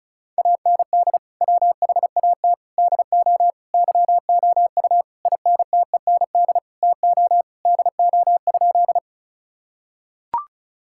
Morse code, heard as a keyed tone, about 34 wpm, 700 Hz. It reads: AND WHAT DO YOU INTEND TO DO?
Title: Sentences from Top 1600 Words 35wpm